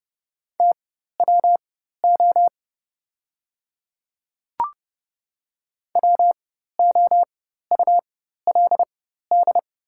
Morse code, heard as a keyed tone, about 30 wpm, spaced as for 14 wpm, 700 Hz. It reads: TWO WOULD
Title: 4x Character Spacing - Top 100 Words - 30wpm